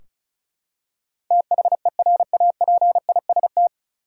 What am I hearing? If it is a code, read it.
Morse code, heard as a keyed tone, about 36 wpm, 700 Hz. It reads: THERAPIST
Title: K1USN SST QSO Practice 35wpm